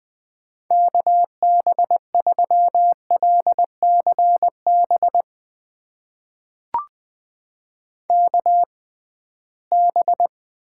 Morse code, heard as a keyed tone, about 20 wpm, spaced as for 15 wpm, 700 Hz. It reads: KB3LCB KB
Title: USA Call Signs - 6 Characters - 6x Character Spacing - 20wpm